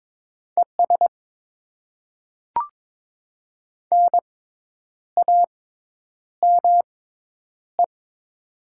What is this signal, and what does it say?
Morse code, heard as a keyed tone, about 22 wpm, spaced as for 9 wpm, 700 Hz. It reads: ES NAME
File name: Top 20 QSO Elements - 6x Character Spacing - 22wpm